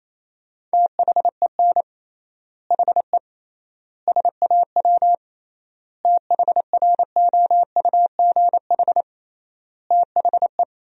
Morse code, heard as a keyed tone, about 28 wpm, 700 Hz. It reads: THEN HE SAW THROUGH THE
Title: 3x Word-Spacing - Sentences from Top 400 Words - 28wpm